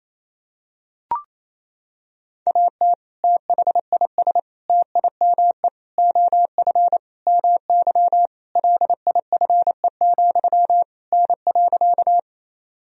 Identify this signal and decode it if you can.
Morse code, heard as a keyed tone, about 28 wpm, 700 Hz. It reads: AT THIS TIME OF MY LIFE, N.
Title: Sentences from Top 300 Words and Abbreviations 28wpm